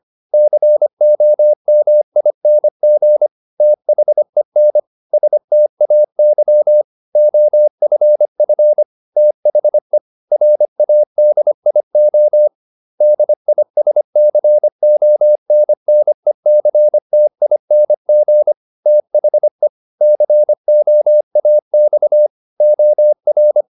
Morse code, heard as a keyed tone, about 25 wpm, 600 Hz. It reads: COMING THEN STAY OFF THE RADIO DISCONNECTING THE COAX OR